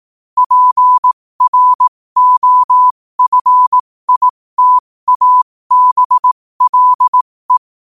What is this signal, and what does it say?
Morse code, heard as a keyed tone, about 18 wpm, spaced as for 15 wpm, 1000 Hz. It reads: PROFITABLE
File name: 15wpm